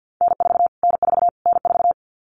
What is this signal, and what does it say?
Morse code, heard as a keyed tone, about 49 wpm, 700 Hz. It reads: N4 N4 N4
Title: Callsign Prefixes for Top 25 DXCC - Repeated 3x - 50wpm